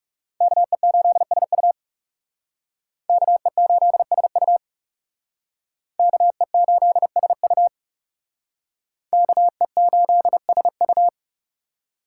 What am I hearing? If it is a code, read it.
Morse code, heard as a keyed tone, about 38 wpm, 700 Hz. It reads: KE8SU KE8SU KE8SU KE8SU
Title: Speed Racing - USA Call Signs 30wpm